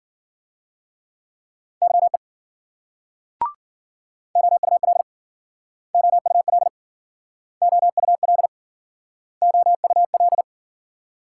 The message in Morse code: YE OUL OUL OUL OUL

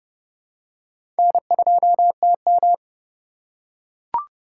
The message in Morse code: N2TM

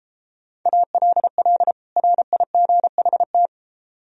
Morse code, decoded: ALL RIGHT

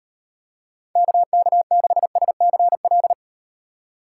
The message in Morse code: KK6SCL

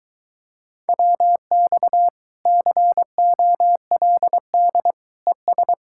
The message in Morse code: WX COLD ES

